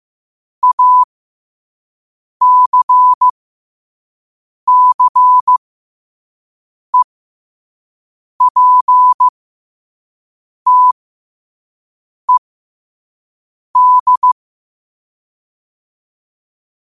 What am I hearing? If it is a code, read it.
Morse code, heard as a keyed tone, about 15 wpm, spaced as for 5 wpm, 1000 Hz. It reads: ACCEPTED
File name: qotdfiles / 5wpm